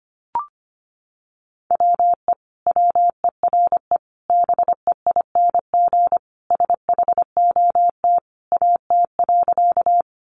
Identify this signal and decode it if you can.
Morse code, heard as a keyed tone, about 25 wpm, 700 Hz. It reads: WE WERE BEING SHOT AT.
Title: Sentences from Top 1100 Words 25wpm